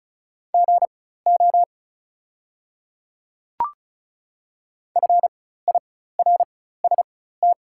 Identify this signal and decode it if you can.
Morse code, heard as a keyed tone, about 36 wpm, spaced as for 16 wpm, 700 Hz. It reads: GO FIRST
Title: 4x Character Spacing - Top 100 Words - 35wpm